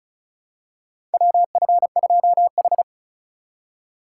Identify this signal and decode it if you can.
Morse code, heard as a keyed tone, about 36 wpm, 690 Hz. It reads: WF2H
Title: USA Call Signs - 4 Characters - 2x Character Spacing - 35wpm